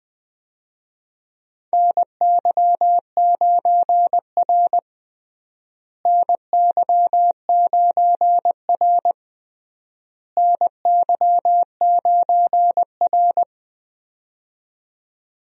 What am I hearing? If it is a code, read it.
Morse code, heard as a keyed tone, about 20 wpm, 700 Hz. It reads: NY9R NY9R NY9R
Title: USA Call Signs - 4 Characters - Repeated 3x - 20wpm